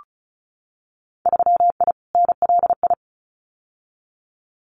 Morse code, heard as a keyed tone, about 36 wpm, 700 Hz. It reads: 3I NLI
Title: Winter Field Day Exchanges 35wpm